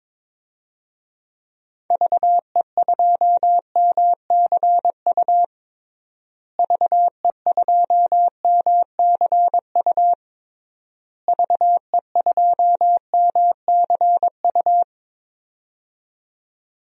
Morse code, heard as a keyed tone, about 22 wpm, 700 Hz. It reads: VE2MCU VE2MCU VE2MCU
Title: DXCC-100 Call Signs - 6 Characters - Repeated 3x - 22wpm